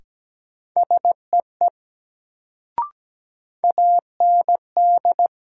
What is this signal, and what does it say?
Morse code, heard as a keyed tone, about 17 wpm, 700 Hz. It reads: SEE AND
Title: Sentences from Top 100 Words - New Word Drill - 17wpm